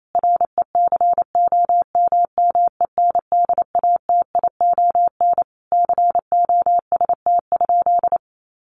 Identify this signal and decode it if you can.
Morse code, heard as a keyed tone, about 28 wpm, 700 Hz. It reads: RECOMMENDATION COST?
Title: Sentences from Top 1400 Words 28wpm